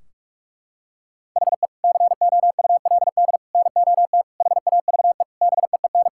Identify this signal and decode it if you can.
Morse code, heard as a keyed tone, about 46 wpm, 700 Hz. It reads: HE COULD NOT HAVE BEEN
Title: Sentences from Top 1600 Words 45wpm